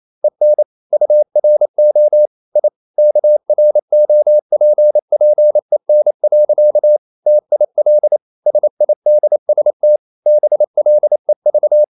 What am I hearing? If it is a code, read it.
Morse code, heard as a keyed tone, about 28 wpm, 600 Hz. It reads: EN URO I KROPPEN. TIL SIDST BLEV
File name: Ferie QRQ 28wpm